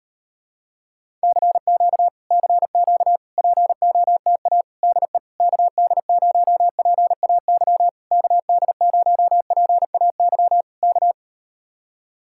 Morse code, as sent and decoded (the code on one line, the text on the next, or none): -.-. --.- / -.-. --.- / .--. --- - .- / -.. / -.- -.. ----- .--. .- -.-- / -.- -.. ----- .--. .- -.-- / -.-
CQ CQ POTA DE KD0PAY KD0PAY K